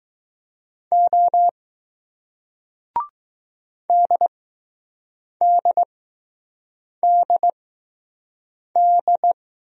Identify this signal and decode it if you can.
Morse code, heard as a keyed tone, about 20 wpm, spaced as for 6 wpm, 700 Hz. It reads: O DDDD